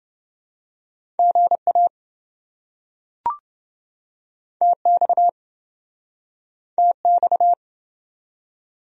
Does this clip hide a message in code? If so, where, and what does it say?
Yes: Morse code from 1.2 to 7.5 s, heard as a keyed tone, about 26 wpm, 700 Hz. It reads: GA TX TX